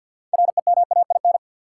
Morse code, heard as a keyed tone, about 50 wpm, 700 Hz. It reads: REMAIN
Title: Sentences from Top 1200 Words - New Word Drill - 50wpm